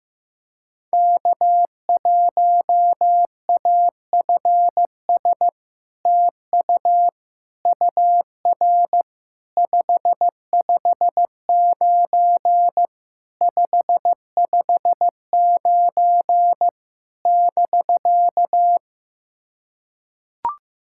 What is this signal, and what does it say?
Morse code, heard as a keyed tone, about 15 wpm, 700 Hz. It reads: K1AFS TU UR 559 559 <BK>
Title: SOTA QSO Practice 15wpm